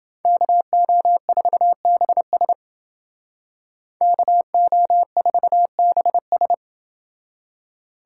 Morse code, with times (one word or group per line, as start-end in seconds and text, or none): 0.3-2.5 KO4BS
4.0-6.5 KO4BS